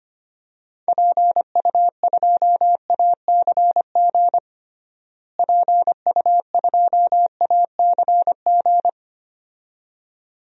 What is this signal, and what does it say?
Morse code, heard as a keyed tone, about 25 wpm, 700 Hz. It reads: PU2ACG PU2ACG
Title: DXCC-100 Call Signs - 6 Characters - Repeated 2x - 25wpm